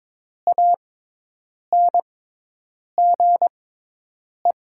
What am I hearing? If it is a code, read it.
Morse code, heard as a keyed tone, about 22 wpm, spaced as for 8 wpm, 700 Hz. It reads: ANGE
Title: 6x Character Spacing - Top 300 Words - 22wpm